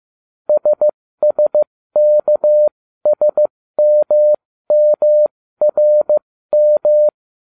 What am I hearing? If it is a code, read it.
Morse code, heard as a keyed tone, about 15 wpm, spaced as for 13 wpm, 600 Hz. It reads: SSKSMMRM